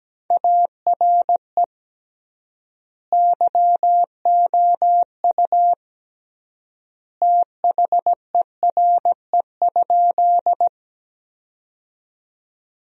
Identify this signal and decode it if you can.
Morse code, heard as a keyed tone, about 17 wpm, 700 Hz. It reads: ARE YOU THERE?
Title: 3x Word-Spacing - Sentences from Top 100 Words - 17wpm